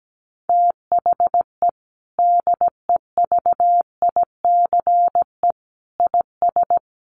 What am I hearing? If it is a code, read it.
Morse code, heard as a keyed tone, about 17 wpm, 700 Hz. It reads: THE DEVICE IS
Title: Sentences from Top 1500 Words 17wpm